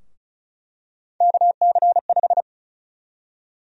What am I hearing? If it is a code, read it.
Morse code, heard as a keyed tone, about 36 wpm, 700 Hz. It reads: KC5
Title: USA Call Sign Prefixes - Repeated 2x - 35wpm